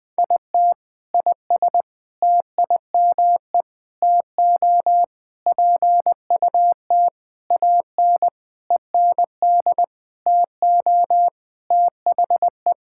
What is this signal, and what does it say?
Morse code, heard as a keyed tone, about 20 wpm, 700 Hz. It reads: IT IS TIME TO PUT AN END TO THE
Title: Sentences with 6-Word Phrases - 20wpm